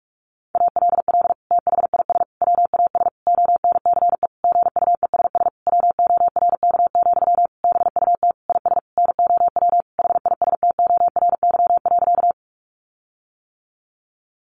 Morse code, heard as a keyed tone, about 44 wpm, 700 Hz. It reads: ALL THIS WAS ONCE GUESS WORK, BUT IS NOW HISTORY.
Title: Sentences from Top 1500 Words 45wpm